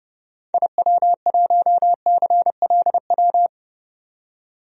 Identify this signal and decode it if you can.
Morse code, heard as a keyed tone, about 32 wpm, 700 Hz. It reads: IW1CLW
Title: DXCC-100 Call Signs - 6 Characters - 30wpm